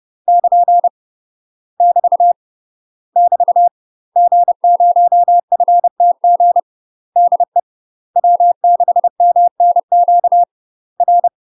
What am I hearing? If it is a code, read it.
Morse code, heard as a keyed tone, about 30 wpm, 700 Hz. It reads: <KN> <BT> <BT> G0FTG DE W6MNQ R
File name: UK MASTER QSOs TRAINING / 30WPM / UK MASTER QSOs 0014